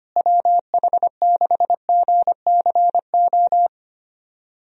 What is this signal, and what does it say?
Morse code, heard as a keyed tone, about 25 wpm, 700 Hz. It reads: WH6GCO
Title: USA Call Signs - 6 Characters - Repeated 3x - 25wpm